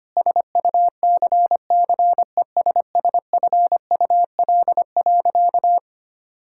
Morse code, heard as a keyed tone, about 25 wpm, 700 Hz. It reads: SUCCESSFUL.
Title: Sentences from Top 1000 Words - Review Part 6 - 25wpm